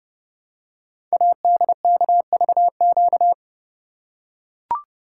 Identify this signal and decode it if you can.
Morse code, heard as a keyed tone, about 30 wpm, 700 Hz. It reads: ADKVQ
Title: Code Groups - 5 Letters - 30wpm